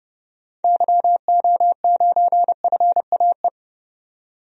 Morse code, heard as a keyed tone, about 31 wpm, 700 Hz. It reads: YO9FAE
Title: DXCC-100 Call Signs - 6 Characters - 30wpm